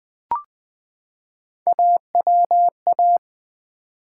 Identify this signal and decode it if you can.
Morse code, heard as a keyed tone, about 20 wpm, 700 Hz. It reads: AWA
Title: Build A Word - 3 Letters At A Time - 9-Letter Words - 20wpm